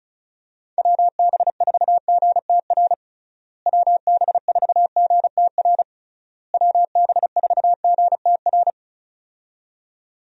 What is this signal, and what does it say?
Morse code, heard as a keyed tone, about 34 wpm, 700 Hz. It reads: WB4GTR WB4GTR WB4GTR
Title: USA Call Signs - 6 Characters - Repeated 3x - 35wpm